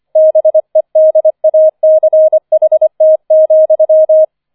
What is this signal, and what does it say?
Morse code, heard as a keyed tone, about 25 wpm, 630 Hz. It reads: BEDACHT,
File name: 005 Fünfter Streich 25wpm